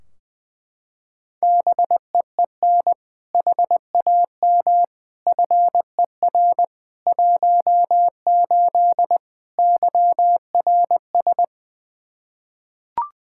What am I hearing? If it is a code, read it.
Morse code, heard as a keyed tone, about 20 wpm, 700 Hz. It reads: BEEN HAM FER 18 YRS
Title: Practice QSO - Exchange 3 - More Info - 20wpm